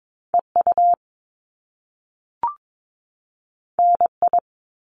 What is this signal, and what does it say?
Morse code, heard as a keyed tone, about 22 wpm, 700 Hz. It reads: EU NI
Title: Top Two Letter Combinations in Words 22wpm